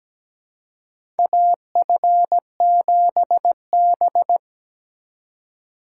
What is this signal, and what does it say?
Morse code, heard as a keyed tone, about 17 wpm, 700 Hz. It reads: AF7B
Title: USA Call Signs - 4 Characters - 3x Character Spacing - 17wpm